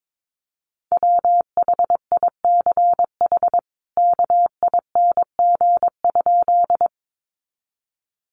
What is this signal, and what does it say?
Morse code, heard as a keyed tone, about 22 wpm, 700 Hz. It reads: WHICH KING?
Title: Sentences from Top 400 Words 22wpm